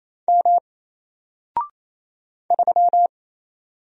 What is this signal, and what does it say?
Morse code, heard as a keyed tone, about 27 wpm, 700 Hz. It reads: M 3
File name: Single-Letter-Number-28wpm